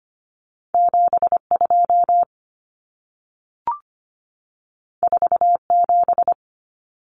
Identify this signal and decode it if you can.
Morse code, heard as a keyed tone, about 25 wpm, 700 Hz. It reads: 72 47
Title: Common Cut Numbers - 2-Digit 25wpm